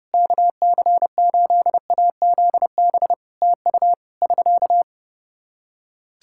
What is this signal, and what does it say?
Morse code, heard as a keyed tone, about 28 wpm, 700 Hz. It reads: KC8AZB TU <SK>
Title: Practice QSO - Exchange 4 - Ending - 30wpm